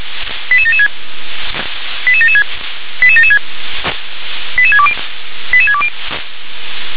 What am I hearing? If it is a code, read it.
Five-tone selective call (ZVEI), heard as quick runs of five tones, tones of about 70 ms. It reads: 88776 88776 88776 88620 88620
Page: SDR/ZVEI